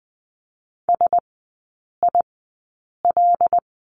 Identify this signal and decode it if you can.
Morse code, heard as a keyed tone, about 20 wpm, spaced as for 8 wpm, 700 Hz. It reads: SIL
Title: 074 Lesson 13e - Letter L - Three Characters 20wpm